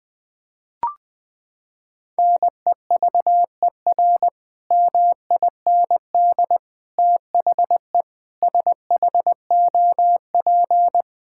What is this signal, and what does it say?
Morse code, heard as a keyed tone, about 20 wpm, 700 Hz. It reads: NEVER MIND THE SHOP
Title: Sentences from Top 1600 Words 20wpm